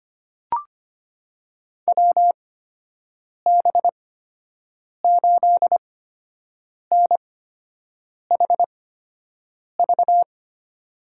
Morse code, heard as a keyed tone, about 25 wpm, spaced as for 7 wpm, 700 Hz. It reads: WB8NHV
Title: USA Call Signs - 6 Characters - 8x Character Spacing - 25wpm